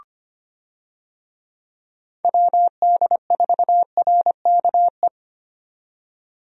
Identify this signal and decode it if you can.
Morse code, heard as a keyed tone, about 25 wpm, 700 Hz. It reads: WD4RKE
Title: USA Call Signs - 6 Characters - 25wpm